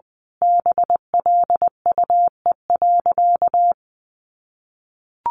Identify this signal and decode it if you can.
Morse code, heard as a keyed tone, about 20 wpm, 700 Hz. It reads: BLUE.
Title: A Sound Of Thunder 20wpm